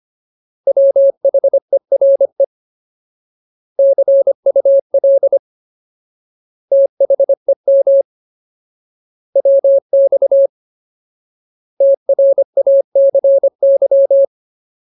Morse code, heard as a keyed tone, about 25 wpm, 550 Hz. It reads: WHERE CUL THEM WX TRACY
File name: Ultimate Word List / 25WPM Ultimate